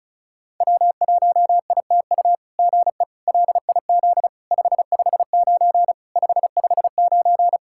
Morse code, heard as a keyed tone, about 33 wpm, 700 Hz. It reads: W1ITU GE LIZ 559 559
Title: SOTA QSO Practice 35wpm